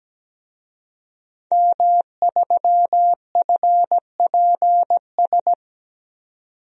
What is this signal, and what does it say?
Morse code, heard as a keyed tone, about 17 wpm, 700 Hz. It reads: M3FPS
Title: DXCC-100 Call Signs - 5 Characters - 17wpm